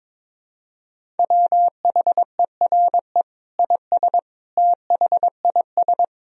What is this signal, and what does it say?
Morse code, heard as a keyed tone, about 21 wpm, 700 Hz. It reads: WHERE IS THIS
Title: Sentences from Top 1000 Words - Review Part 3 - 22wpm